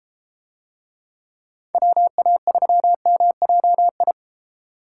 Morse code, heard as a keyed tone, about 33 wpm, 700 Hz. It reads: WA3MJI